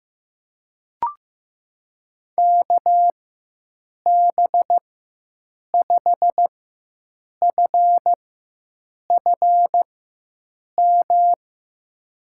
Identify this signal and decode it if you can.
Morse code, heard as a keyed tone, about 16 wpm, spaced as for 7 wpm, 700 Hz. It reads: KB5FFM
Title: USA Call Signs - 6 Characters - 4x Character Spacing - 15wpm